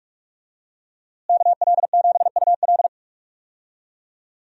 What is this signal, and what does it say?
Morse code, heard as a keyed tone, about 43 wpm, 700 Hz. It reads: KL7UL
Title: USA Call Signs - 5 Characters - Repeated 3x - 45wpm